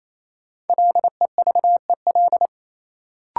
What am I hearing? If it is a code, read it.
Morse code, heard as a keyed tone, about 28 wpm, 700 Hz. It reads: LEVEL
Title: Sentences from Top 700 Words - New Word Drill - 28wpm